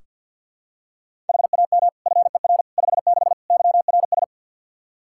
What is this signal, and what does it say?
Morse code, heard as a keyed tone, about 47 wpm, 700 Hz. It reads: HAM FER 56 YRS
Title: Practice QSO - Exchange 3 - More Info - 50wpm